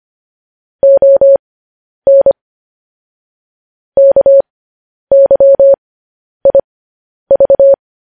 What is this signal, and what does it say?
Morse code, heard as a keyed tone, about 25 wpm, spaced as for 10 wpm, 550 Hz. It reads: ON KYIV
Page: news headlines in morse code at 10 wpm